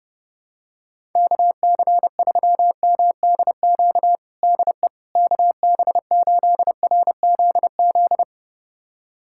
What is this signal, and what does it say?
Morse code, heard as a keyed tone, about 30 wpm, 700 Hz. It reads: KC3MDQ DE KB8RZZ
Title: USA Callsign DE Callsign 30wpm